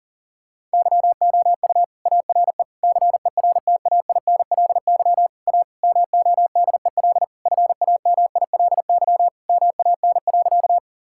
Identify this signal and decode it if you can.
Morse code, heard as a keyed tone, about 40 wpm, 700 Hz. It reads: YOU ARE CERTAINLY A MODEL FAMILY MAN.
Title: Sentences from Top 1000 Words - Review Part 5 - 40wpm